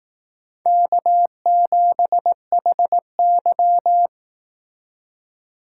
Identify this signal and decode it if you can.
Morse code, heard as a keyed tone, about 18 wpm, 700 Hz. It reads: K7HY
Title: Speed Racing - USA 1x2 Call Signs 15wpm